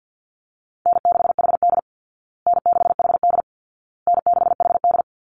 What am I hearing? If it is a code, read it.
Morse code, heard as a keyed tone, about 50 wpm, 700 Hz. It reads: N6HD N6HD N6HD
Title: Easy Call Signs Repeated 3x - 50wpm